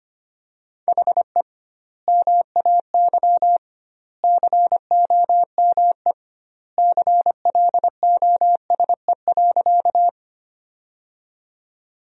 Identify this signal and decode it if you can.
Morse code, heard as a keyed tone, about 25 wpm, 700 Hz. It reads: HE MAY COME CLOSE.
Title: 2x Word-Spacing - Sentences from Top 300 Words - 25wpm